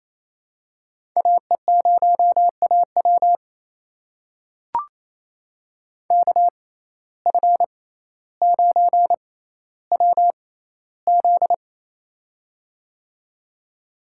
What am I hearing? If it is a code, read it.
Morse code, heard as a keyed tone, about 28 wpm, spaced as for 14 wpm, 700 Hz. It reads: AE0AW KF9WZ
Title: USA Call Signs - 5 Characters - 6x Character Spacing - 28wpm